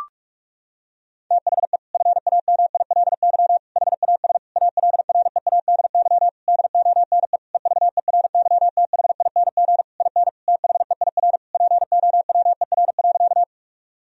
Morse code, heard as a keyed tone, about 46 wpm, 700 Hz. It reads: THE FAMILY HAS ALREADY DONE EVERYTHING IN THEIR POWER.
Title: Sentences from Top 1000 Words - Review Part 2 - 45wpm